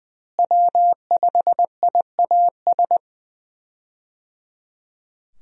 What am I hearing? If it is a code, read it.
Morse code, heard as a keyed tone, about 20 wpm, 700 Hz. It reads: W5IAS
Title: USA Call Signs - 5 Characters - Repeated 3x - 20wpm